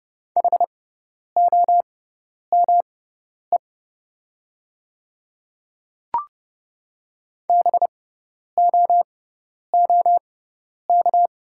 Morse code, heard as a keyed tone, about 30 wpm, spaced as for 10 wpm, 700 Hz. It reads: HOME BOOK
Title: 6x Character Spacing - Top 300 Words - 30wpm